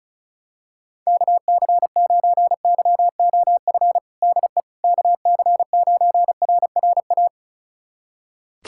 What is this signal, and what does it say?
Morse code, heard as a keyed tone, about 36 wpm, 700 Hz. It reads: KC9YOF DE KC9RRA
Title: USA Callsign DE Callsign 35wpm